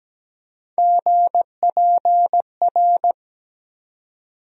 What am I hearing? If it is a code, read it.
Morse code, heard as a keyed tone, about 17 wpm, 700 Hz. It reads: GPR
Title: USA Call Sign Postfixes - 3 Letters 17wpm